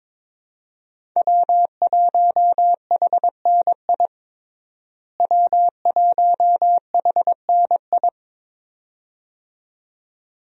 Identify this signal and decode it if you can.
Morse code, heard as a keyed tone, about 22 wpm, 700 Hz. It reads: W1HNI W1HNI
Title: USA Call Signs - 5 Characters - Repeated 2x - 22wpm